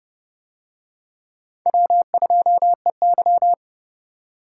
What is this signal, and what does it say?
Morse code, heard as a keyed tone, about 30 wpm, 700 Hz. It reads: W2EY